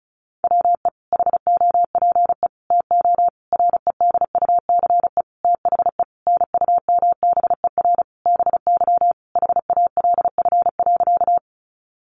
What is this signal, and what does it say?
Morse code, heard as a keyed tone, about 34 wpm, 700 Hz. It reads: WE HOPE TO REDUCE THE NUMBER BY HALF.
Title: Sentences from Top 1100 Words 35wpm